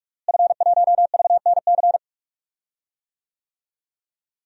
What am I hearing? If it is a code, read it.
Morse code, heard as a keyed tone, about 43 wpm, 700 Hz. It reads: F1VNC
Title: DXCC-100 Call Signs - 5 Characters - Repeated 3x - 45wpm